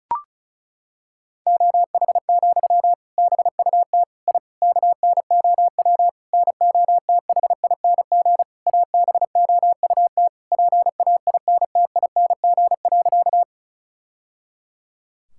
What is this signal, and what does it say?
Morse code, heard as a keyed tone, about 34 wpm, 700 Hz. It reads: OH, BUT I KNOW NOTHING ABOUT PAINTING.
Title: Sentences from Top 1000 Words - Review Part 6 - 35wpm